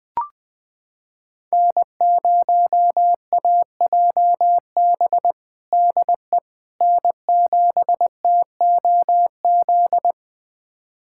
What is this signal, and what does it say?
Morse code, heard as a keyed tone, about 20 wpm, 700 Hz. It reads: N0AJB DE N7TOZ
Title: USA Callsign DE Callsign - 5 Characters - 20wpm